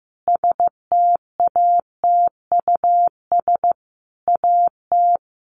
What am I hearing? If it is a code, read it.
Morse code, heard as a keyed tone, about 15 wpm, 700 Hz. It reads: STATUS AT
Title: Sentences from Top 1400 Words 15wpm